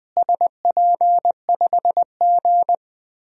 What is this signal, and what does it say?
Morse code, heard as a keyed tone, about 21 wpm, 700 Hz. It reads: SP5G
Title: DXCC-100 Call Signs - 4 Characters - Repeated 3x - 20wpm